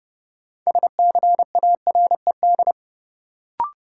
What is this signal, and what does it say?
Morse code, heard as a keyed tone, about 29 wpm, 700 Hz. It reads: SCARED
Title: Sentences from Top 700 Words - New Word Drill - 30wpm